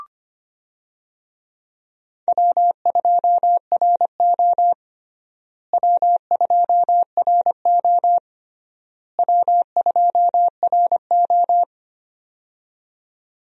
Morse code, heard as a keyed tone, about 25 wpm, 700 Hz. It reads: W2RO W2RO W2RO
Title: USA Call Signs - 4 Characters - Repeated 3x - 25wpm